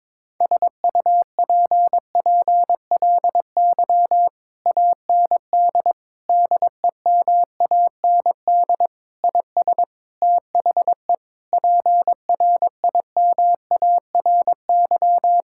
Morse code, heard as a keyed tone, about 22 wpm, 700 Hz. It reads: SUPPLY AND DEMAND IS THE PRIMARY
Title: Sentences from Top 1000 Words - Review Part 6 - 22wpm